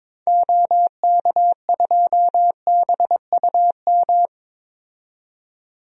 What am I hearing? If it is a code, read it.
Morse code, heard as a keyed tone, about 22 wpm, 700 Hz. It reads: OK2BUM
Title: DXCC-100 Call Signs - 6 Characters - 22wpm